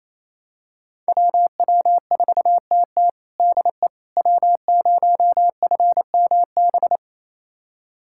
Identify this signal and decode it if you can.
Morse code, heard as a keyed tone, about 28 wpm, 700 Hz. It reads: WW4TT DE W0FMB
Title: USA Callsign DE Callsign - 5 Characters - 28wpm